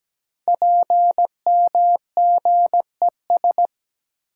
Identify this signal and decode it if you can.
Morse code, heard as a keyed tone, about 17 wpm, 700 Hz. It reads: PMGES
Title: Code Groups - 5 Letters - 17wpm